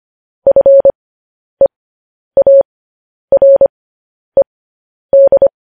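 Morse code, heard as a keyed tone, about 25 wpm, spaced as for 10 wpm, 550 Hz. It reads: FEARED
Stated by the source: Morse news headlines sent at 10 wpm.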